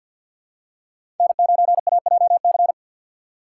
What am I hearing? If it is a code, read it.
Morse code, heard as a keyed tone, about 51 wpm, 700 Hz. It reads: N9AJC
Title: USA Call Signs - 3x Character Spacing - 50wpm